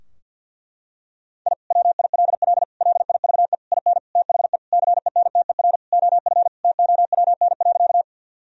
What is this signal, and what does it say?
Morse code, heard as a keyed tone, about 47 wpm, 700 Hz. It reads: I WILL LIVE IN THE CENTER OF TOWN.
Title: Sentences from Top 1000 Words - Review Part 2 - 50wpm